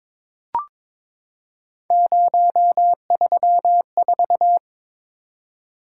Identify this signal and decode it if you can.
Morse code, heard as a keyed tone, about 23 wpm, 700 Hz. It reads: 034
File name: practice / 22wpm / Three Letter-Number Combinations - 22wpm